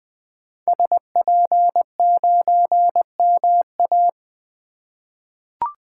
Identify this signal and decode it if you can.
Morse code, heard as a keyed tone, about 21 wpm, 700 Hz. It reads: SP9MA
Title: 2023 SOTA Activator Call Signs - 20wpm